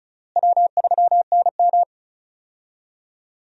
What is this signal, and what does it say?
Morse code, heard as a keyed tone, about 36 wpm, 700 Hz. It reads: W3NM
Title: USA Call Signs - 4 Characters - Repeated 3x - 35wpm